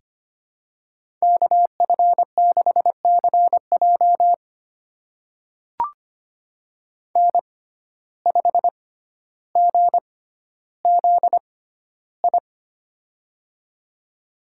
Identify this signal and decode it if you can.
Morse code, heard as a keyed tone, about 25 wpm, spaced as for 13 wpm, 700 Hz. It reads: KF6CJ N5GZI
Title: USA Call Signs - 5 Characters - 6x Character Spacing - 25wpm